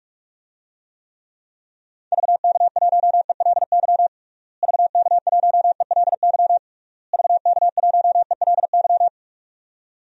Morse code, heard as a keyed tone, about 41 wpm, 700 Hz. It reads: VK1ELY VK1ELY VK1ELY
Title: DXCC-100 Call Signs - 6 Characters - Repeated 3x - 45wpm